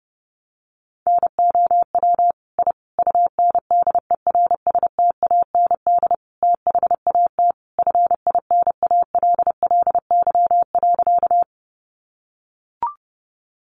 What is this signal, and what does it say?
Morse code, heard as a keyed tone, about 30 wpm, 700 Hz. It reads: NOW I UNDERSTAND THAT FINALLY.
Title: Sentences from Top 1000 Words 30wpm